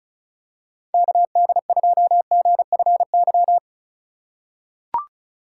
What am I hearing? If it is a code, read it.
Morse code, heard as a keyed tone, about 36 wpm, 700 Hz. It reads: KD2GFY
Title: USA Call Signs - 6 Characters - 8x Character Spacing - 35wpm